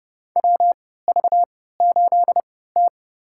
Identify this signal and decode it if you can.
Morse code, heard as a keyed tone, about 30 wpm, spaced as for 17 wpm, 700 Hz. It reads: WV8T
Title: USA Call Signs - 4 Characters - 3x Character Spacing - 30wpm